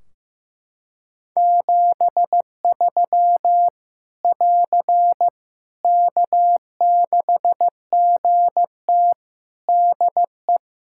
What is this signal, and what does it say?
Morse code, heard as a keyed tone, about 15 wpm, 700 Hz. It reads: 73 <AR> K6GT DE
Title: Practice QSO - Exchange 4 - Ending - 15wpm